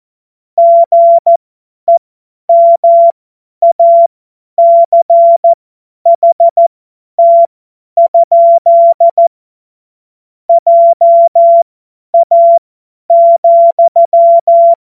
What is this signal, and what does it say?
Morse code, heard as a keyed tone, about 14 wpm, spaced as for 10 wpm, 680 Hz. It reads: GEMACHT? JA,
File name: Morsekurs / Witze / Witz 05. Oktober 2021, 50 F70